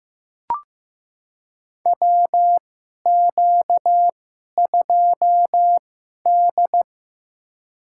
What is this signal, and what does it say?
Morse code, heard as a keyed tone, about 15 wpm, spaced as for 11 wpm, 700 Hz. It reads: WQ2D
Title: USA Call Signs - 2x Character Spacing - 15wpm